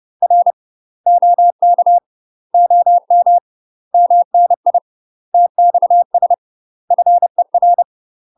Morse code, heard as a keyed tone, about 30 wpm, 700 Hz. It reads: R OK OM MNI TXS FER
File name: UK MASTER QSOs TRAINING / 30WPM / UK MASTER QSOs 0044